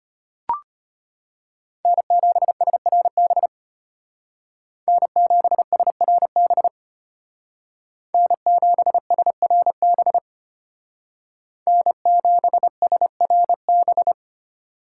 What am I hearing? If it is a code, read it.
Morse code, heard as a keyed tone, about 31 wpm, 700 Hz. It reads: N7SRB N7SRB N7SRB N7SRB
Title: Speed Racing - USA 1x3 Call Signs 25wpm